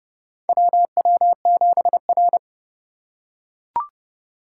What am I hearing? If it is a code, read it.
Morse code, heard as a keyed tone, about 31 wpm, 700 Hz. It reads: WW7R
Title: USA Call Signs - 4 Characters - 3x Character Spacing - 30wpm